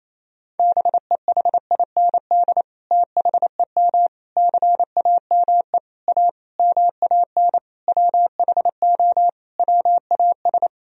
Morse code, heard as a keyed tone, about 27 wpm, 700 Hz. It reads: BEHIND THEM CAME A MAN WHO WAS
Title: Sentences from Top 1000 Words - Review Part 4 - 28wpm